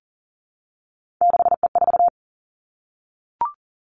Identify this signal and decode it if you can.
Morse code, heard as a keyed tone, about 40 wpm, 700 Hz. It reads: 6E4
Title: Three Challenging Letter-Number Combinations 40wpm